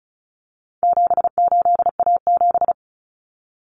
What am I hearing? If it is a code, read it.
Morse code, heard as a keyed tone, about 34 wpm, 700 Hz. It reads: 78A7
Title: Common Cut Numbers - 4-Digit 35wpm